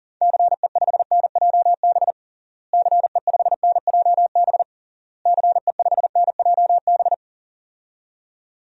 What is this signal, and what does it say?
Morse code, heard as a keyed tone, about 37 wpm, 700 Hz. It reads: CE5NJB CE5NJB CE5NJB
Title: DXCC-100 Call Signs - 6 Characters - Repeated 3x - 40wpm